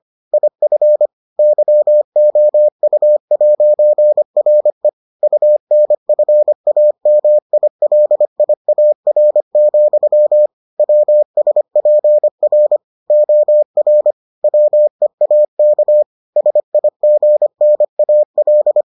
Morse code, heard as a keyed tone, about 25 wpm, 600 Hz. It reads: IF YOU'RE UNFAMILIAR, WSPR OR WEAK SIGNAL